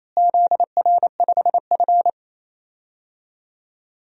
Morse code, heard as a keyed tone, about 28 wpm, 700 Hz. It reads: ZR5F
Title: DXCC-100 Call Signs - 4 Characters - Repeated 3x - 28wpm